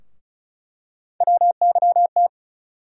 Morse code, heard as a keyed tone, about 36 wpm, 700 Hz. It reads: WYT
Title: USA Call Sign Postfixes - 3 Letters 35wpm